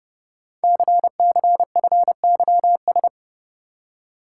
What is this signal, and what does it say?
Morse code, heard as a keyed tone, about 30 wpm, 700 Hz. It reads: CCFYS
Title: Code Groups - 5 Letters - 30wpm